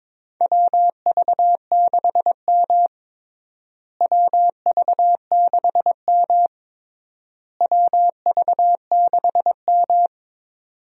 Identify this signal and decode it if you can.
Morse code, heard as a keyed tone, about 22 wpm, 700 Hz. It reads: WV6M WV6M WV6M
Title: USA Call Signs - 4 Characters - Repeated 3x - 22wpm